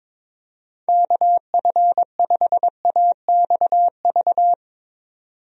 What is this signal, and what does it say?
Morse code, heard as a keyed tone, about 22 wpm, 700 Hz. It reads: KF5AXV
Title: USA Call Signs - 6 Characters - 22wpm